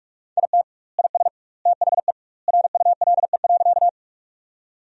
Morse code, heard as a keyed tone, about 46 wpm, 700 Hz. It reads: IT IS THE RULE.
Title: 2x Word-Spacing - Sentences from Top 400 Words - 45wpm